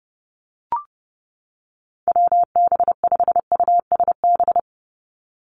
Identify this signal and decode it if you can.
Morse code, heard as a keyed tone, about 30 wpm, 700 Hz. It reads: WB5USB
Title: NA SOTA Activator Callsigns 30wpm